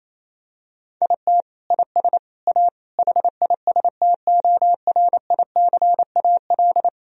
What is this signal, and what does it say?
Morse code, heard as a keyed tone, about 28 wpm, 700 Hz. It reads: IT IS A HISTORICAL
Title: Sentences from Top 1000 Words - Review Part 4 - 28wpm